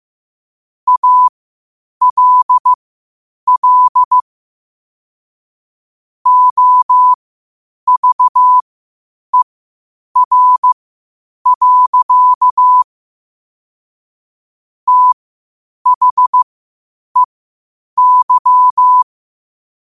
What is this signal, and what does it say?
Morse code, heard as a keyed tone, about 15 wpm, spaced as for 8 wpm, 1000 Hz. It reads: ALL OVER. THEY
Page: New quotes every day in morse code at 8 Words per minute.